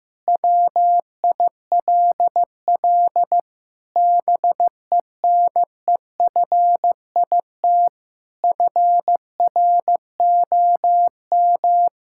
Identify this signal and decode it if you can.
Morse code, heard as a keyed tone, about 16 wpm, 700 Hz. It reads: WILL BENEFIT FROM
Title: Sentences from Top 1000 Words - Review Part 6 - 15wpm